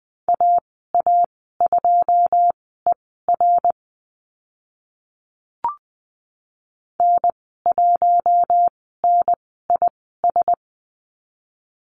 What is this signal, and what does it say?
Morse code, heard as a keyed tone, about 20 wpm, spaced as for 14 wpm, 700 Hz. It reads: AA2ER N1NIS
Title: 082 Lesson 14g - Number 2 - Call Signs 20wpm